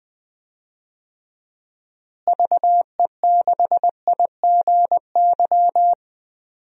Morse code, heard as a keyed tone, about 20 wpm, 700 Hz. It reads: VE6IGY
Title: DXCC-100 Call Signs - 6 Characters - Repeated 2x - 20wpm